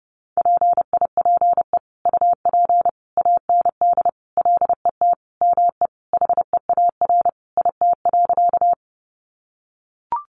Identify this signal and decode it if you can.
Morse code, heard as a keyed tone, about 30 wpm, 700 Hz. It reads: PIPE UP AND LET ME HEAR IT.
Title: Sentences from Top 1500 Words 30wpm